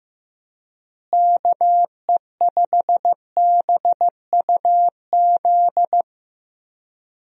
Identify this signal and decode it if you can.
Morse code, heard as a keyed tone, about 16 wpm, 700 Hz. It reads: KE5BUZ
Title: USA Call Signs - 6 Characters - Repeated 2x - 15wpm